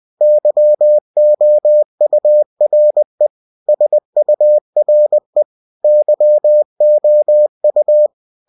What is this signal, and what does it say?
Morse code, heard as a keyed tone, about 20 wpm, 600 Hz. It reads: YOURE SURE YOU
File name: cw-books / winnie-the-pooh / winnie-20wpm-page-0016